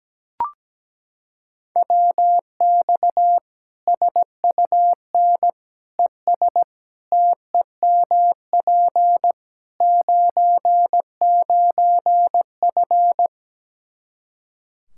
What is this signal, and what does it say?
Morse code, heard as a keyed tone, about 17 wpm, 700 Hz. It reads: WX SUN ES TEMP 99F
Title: Practice QSO - Exchange 2 - Info - 17wpm